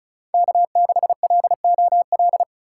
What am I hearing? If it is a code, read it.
Morse code, heard as a keyed tone, about 36 wpm, 700 Hz. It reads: K6LOL
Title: USA Call Signs - 5 Characters - 3x Character Spacing - 35wpm